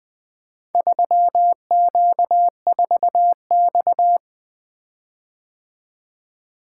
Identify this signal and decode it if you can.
Morse code, heard as a keyed tone, about 20 wpm, 700 Hz. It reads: 3Q4X
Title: Four Letter-Number Combinations - 20wpm